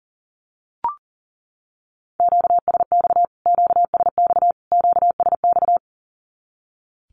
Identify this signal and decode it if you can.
Morse code, heard as a keyed tone, about 40 wpm, 700 Hz. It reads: QSX QSX QSX
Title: Q-Codes Repeated 3x 40wpm